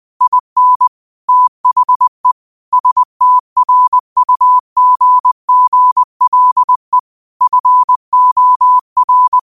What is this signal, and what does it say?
Morse code, heard as a keyed tone, about 20 wpm, 1000 Hz. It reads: IN THE STRUGGLE FOR